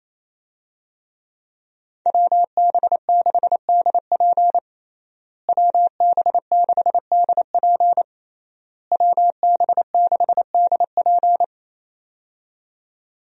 Morse code, heard as a keyed tone, about 28 wpm, 700 Hz. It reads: WB6DP WB6DP WB6DP
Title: USA Call Signs - 5 Characters - Repeated 3x - 28wpm